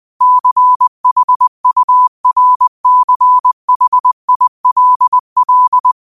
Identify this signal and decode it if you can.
Morse code, heard as a keyed tone, about 20 wpm, 1000 Hz. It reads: CHURCHILL
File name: qotdfiles / 20wpm